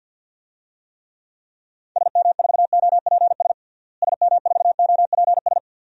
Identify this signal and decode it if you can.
Morse code, heard as a keyed tone, about 50 wpm, 700 Hz. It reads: SM4OPS SM4OPS
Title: DXCC-100 Call Signs - 6 Characters - Repeated 3x - 50wpm